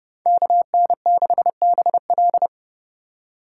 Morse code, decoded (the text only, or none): KN6BL